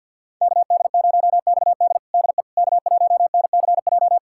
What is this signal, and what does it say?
Morse code, heard as a keyed tone, about 51 wpm, 700 Hz. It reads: KD0XD DE K1NCJ
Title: USA Callsign DE Callsign - 5 Characters - 50wpm